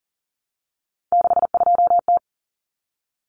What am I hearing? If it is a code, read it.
Morse code, heard as a keyed tone, about 40 wpm, 700 Hz. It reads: 62T
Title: Common Cut Numbers - 3-Digit 40wpm